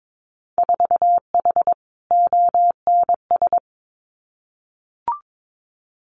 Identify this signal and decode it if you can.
Morse code, heard as a keyed tone, about 22 wpm, 700 Hz. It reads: 4H ONS
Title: Winter Field Day Exchanges Encore 22wpm